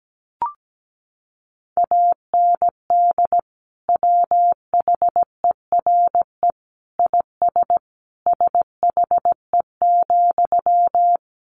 Morse code, heard as a keyed tone, about 17 wpm, 700 Hz. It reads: AND WHERE IS SHE,
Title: Sentences from Top 200 Words 17wpm